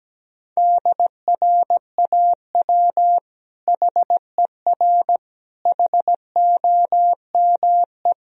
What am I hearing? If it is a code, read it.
Morse code, heard as a keyed tone, about 17 wpm, 700 Hz. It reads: DRAW HER HOME
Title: Sentences from Top 1000 Words - Review Part 1 - 17wpm